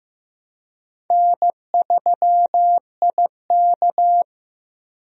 Morse code, heard as a keyed tone, about 15 wpm, 700 Hz. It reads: N3IK
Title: USA Call Signs - 4 Characters - Repeated 3x - 15wpm